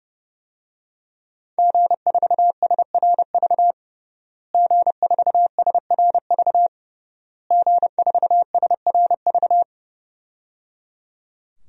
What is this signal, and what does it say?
Morse code, heard as a keyed tone, about 29 wpm, 700 Hz. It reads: G4SRV G4SRV G4SRV
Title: DXCC-100 Call Signs - 5 Characters - Repeated 3x - 30wpm